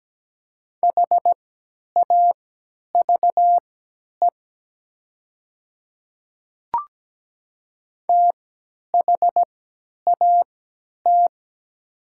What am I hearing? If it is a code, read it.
Morse code, heard as a keyed tone, about 17 wpm, spaced as for 10 wpm, 690 Hz. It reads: HAVE THAT
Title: 3x Character Spacing - Top 100 Words - 17wpm